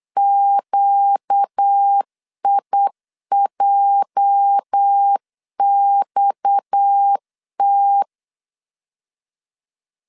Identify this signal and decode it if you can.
Morse code, heard as a keyed tone, about 8 wpm, 800 Hz. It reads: QIJXT
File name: LEZIONI / 8 YJ45 / 8 YJ 8wpm 3313